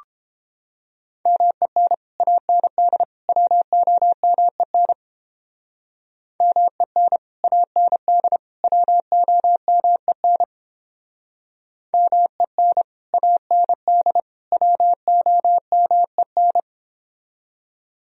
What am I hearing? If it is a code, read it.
Morse code, heard as a keyed tone, about 30 wpm, 700 Hz. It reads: MEN AND WOMEN MEN AND WOMEN MEN AND WOMEN